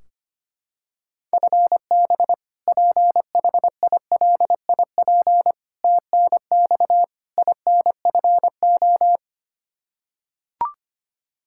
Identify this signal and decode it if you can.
Morse code, heard as a keyed tone, about 25 wpm, 700 Hz. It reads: FB PHILIP TNX INFO
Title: Practice QSO - Exchange 3 - More Info - 25wpm